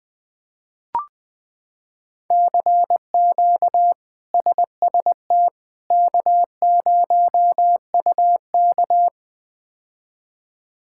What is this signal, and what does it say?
Morse code, heard as a keyed tone, about 20 wpm, 700 Hz. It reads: CQ SST K0UK
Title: K1USN SST QSO Practice - 20wpm